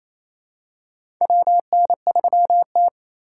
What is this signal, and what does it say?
Morse code, heard as a keyed tone, about 28 wpm, 700 Hz. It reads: WN3T
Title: USA Call Signs - 4 Characters - 4x Character Spacing - 28wpm